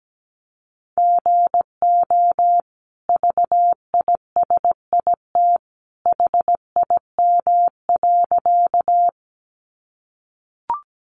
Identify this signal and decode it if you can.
Morse code, heard as a keyed tone, about 17 wpm, 700 Hz. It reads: GO VISIT HIM.
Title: Sentences from Top 900 Words 17wpm